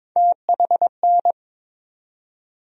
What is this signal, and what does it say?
Morse code, heard as a keyed tone, about 22 wpm, 700 Hz. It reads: THN
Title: USA Call Sign Postfixes - 3 Letters 22wpm